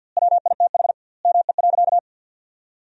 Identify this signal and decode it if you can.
Morse code, heard as a keyed tone, about 50 wpm, 700 Hz. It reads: WITH ME.
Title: 2x Word-Spacing - Sentences from Top 200 Words - 50wpm